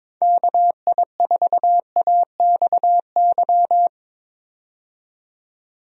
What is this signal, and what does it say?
Morse code, heard as a keyed tone, about 22 wpm, 700 Hz. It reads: KI4AXY
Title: USA Call Signs - 6 Characters - 22wpm